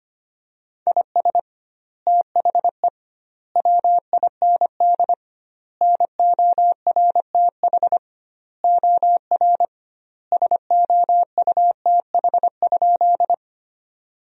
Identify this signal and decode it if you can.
Morse code, heard as a keyed tone, about 25 wpm, 700 Hz. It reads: IS THE WIND NORTH OR SOUTH?
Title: 2x Word-Spacing - Sentences from Top 400 Words - 25wpm